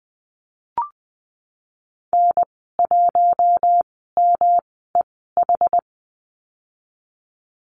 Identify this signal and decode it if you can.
Morse code, heard as a keyed tone, about 20 wpm, spaced as for 15 wpm, 700 Hz. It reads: N1MEH
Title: 140 Lesson 23g - Letter f - Call Signs 20wpm